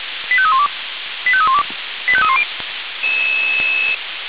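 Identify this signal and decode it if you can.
Five-tone selective call (ZVEI), heard as quick runs of five tones, tones of about 70 ms. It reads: 85312 85312 85310
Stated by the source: ZVEI five-tone call received by SDR, from feuerwehr-orlamuende.org fire brigade audio server.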